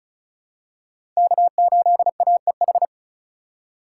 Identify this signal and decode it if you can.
Morse code, heard as a keyed tone, about 36 wpm, 700 Hz. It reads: K8AEH
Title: USA Call Signs - 5 Characters - 3x Character Spacing - 35wpm